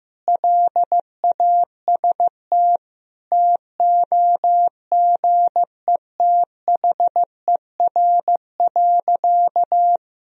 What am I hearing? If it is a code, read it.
Morse code, heard as a keyed tone, about 15 wpm, 700 Hz. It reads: LAST TOGETHER.
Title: Sentences from Top 1000 Words - Review Part 1 - 15wpm